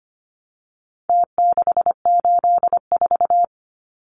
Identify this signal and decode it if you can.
Morse code, heard as a keyed tone, about 25 wpm, 700 Hz. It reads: T684
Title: Numbers - Contest Style - 4-Digit 25wpm